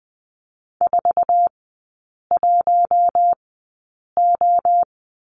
Morse code, heard as a keyed tone, about 20 wpm, spaced as for 8 wpm, 700 Hz. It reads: 41O
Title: 047 Lesson 9e - Number 4 - Three Characters 20wpm